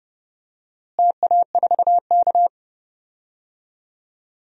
Morse code, heard as a keyed tone, about 30 wpm, 700 Hz. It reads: TA4K
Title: DXCC-100 Call Signs - 4 Characters - Repeated 2x - 30wpm